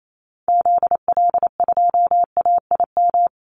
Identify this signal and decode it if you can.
Morse code, heard as a keyed tone, about 28 wpm, 700 Hz. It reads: ZL2AIM
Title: Call Signs Repeated 2x - 28wpm